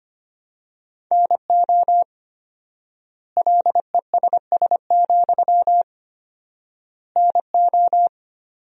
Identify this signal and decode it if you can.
Morse code, heard as a keyed tone, about 25 wpm, 700 Hz. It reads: NO LESS, NO
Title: 4x Word-Spacing - Sentences from Top 400 Words - 25wpm